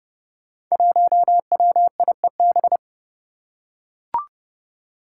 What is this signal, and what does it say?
Morse code, heard as a keyed tone, about 30 wpm, 700 Hz. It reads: 1WIEB
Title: Five Challenging Letter-Number Combinations 30wpm